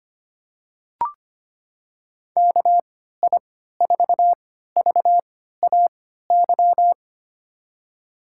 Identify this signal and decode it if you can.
Morse code, heard as a keyed tone, about 25 wpm, spaced as for 14 wpm, 700 Hz. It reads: KI4VAY
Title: USA Call Signs - 6 Characters - 3x Character Spacing - 25wpm